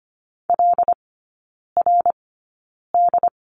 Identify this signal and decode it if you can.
Morse code, heard as a keyed tone, about 25 wpm, spaced as for 9 wpm, 700 Hz. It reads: LRD
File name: Lesson 14e - Number 2 - Three Characters 25wpm